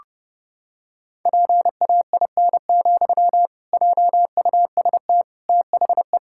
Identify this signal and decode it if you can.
Morse code, heard as a keyed tone, about 30 wpm, 700 Hz. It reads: PAIN, JUST THE
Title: Sentences from Top 1600 Words 30wpm